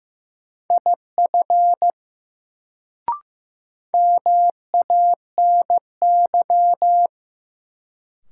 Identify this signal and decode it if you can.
Morse code, heard as a keyed tone, about 15 wpm, 700 Hz. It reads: IF MANY
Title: Sets of 2 Words - New Word Drill - 15wpm